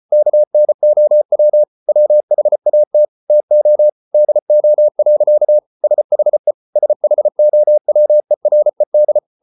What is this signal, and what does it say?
Morse code, heard as a keyed tone, about 34 wpm, 600 Hz. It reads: KNOW WHAT TO DO. SHE SHOWERED